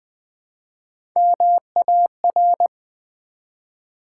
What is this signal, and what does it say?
Morse code, heard as a keyed tone, about 20 wpm, 700 Hz. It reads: MAR
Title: Build A Word - 1 Letter At A Time - 3 to 5-Letter Words - 20wpm